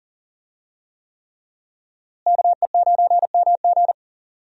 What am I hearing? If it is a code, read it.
Morse code, heard as a keyed tone, about 40 wpm, 700 Hz. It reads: KE9MG
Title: USA Call Signs - 5 Characters - Repeated 3x - 40wpm